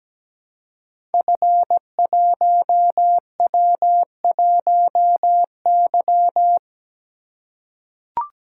Morse code, heard as a keyed tone, about 17 wpm, 700 Hz. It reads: F1W1Y
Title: Code Groups - 5 Letter:Numbers - 17wpm